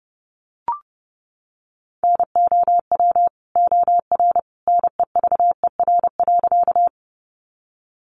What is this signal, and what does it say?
Morse code, heard as a keyed tone, about 30 wpm, 700 Hz. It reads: NOW OR NEVER.
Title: Sentences from Top 300 Words 30wpm